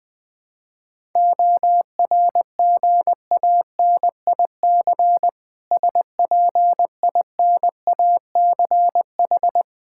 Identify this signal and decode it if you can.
Morse code, heard as a keyed tone, about 20 wpm, 690 Hz. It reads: ORGANIC SPINACH
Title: Difficult Words with Hints - 20wpm